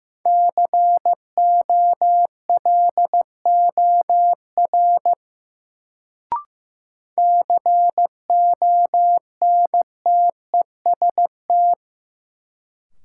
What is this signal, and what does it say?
Morse code, heard as a keyed tone, about 15 wpm, 700 Hz. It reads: COLOR CONTEST
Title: Sentences from Top 1600 Words - New Word Drill Alt - 15wpm